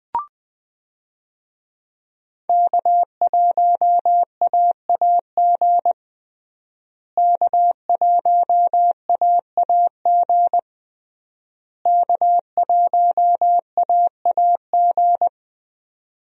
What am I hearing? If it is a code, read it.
Morse code, heard as a keyed tone, about 21 wpm, 700 Hz. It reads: K1AAG K1AAG K1AAG
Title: USA Call Signs - 5 Characters - Repeated 3x - 20wpm